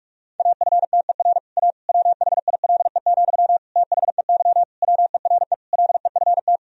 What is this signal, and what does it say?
Morse code, heard as a keyed tone, about 41 wpm, 700 Hz. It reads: AFTER A WHILE, THEY WERE LEFT
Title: Sentences from Top 1000 Words - Review Part 1 - 45wpm